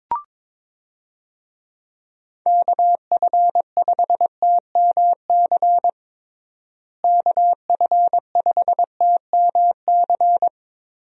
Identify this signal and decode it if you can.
Morse code, heard as a keyed tone, about 21 wpm, 700 Hz. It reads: KF5TMC KF5TMC
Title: USA Call Signs - 6 Characters - Repeated 3x - 22wpm